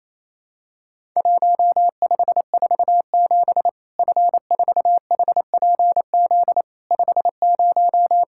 Morse code, heard as a keyed tone, about 28 wpm, 700 Hz. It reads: 1547 F4HPZ 50
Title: Worked All Europe DX Contest - QTC Exchange - 28wpm